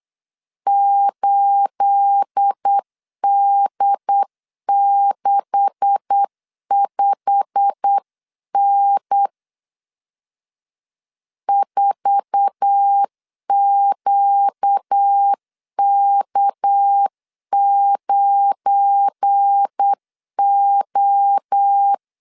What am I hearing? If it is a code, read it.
Morse code, heard as a keyed tone, about 8 wpm, 800 Hz. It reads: 8D65N 4QK9O
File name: CW / DET / 1 9 1500C S A 8wpm 5637